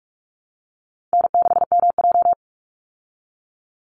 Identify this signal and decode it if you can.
Morse code, heard as a keyed tone, about 46 wpm, 700 Hz. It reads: N6MJ
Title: Easy Call Signs Repeated 3x - 45wpm